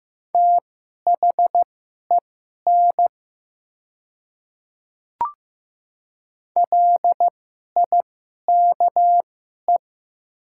Morse code, heard as a keyed tone, about 15 wpm, spaced as for 11 wpm, 700 Hz. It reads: THEN LIKE
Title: 2x Character Spacing - Top 100 Words - 15wpm